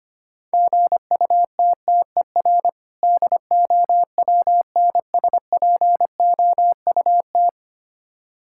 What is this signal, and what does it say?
Morse code, heard as a keyed tone, about 26 wpm, 700 Hz. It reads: GUTTER DOWNSPOUT
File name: practice / 25wpm / Difficult Words - Speed-Racing with Hints - 25wpm